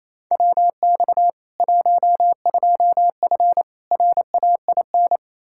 Morse code, heard as a keyed tone, about 28 wpm, 700 Hz. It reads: WX 12F RAIN
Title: K1USN SST QSO Practice 28wpm